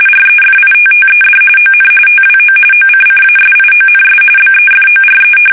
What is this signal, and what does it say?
RTTY, about 45.45 baud. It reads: REFITTED ON TWO OCCASIONS BEFORE
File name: fsk850 / battleships